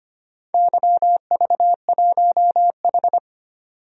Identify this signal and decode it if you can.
Morse code, heard as a keyed tone, about 25 wpm, 700 Hz. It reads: YV1H
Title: DXCC-100 Call Signs - 4 Characters - 25wpm